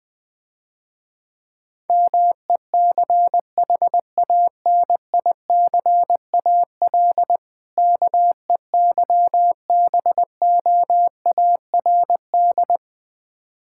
Morse code, heard as a keyed tone, about 20 wpm, 700 Hz. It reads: MECHANICAL KEYBOARD
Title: Difficult Words - Speed-Racing with Hints - 20wpm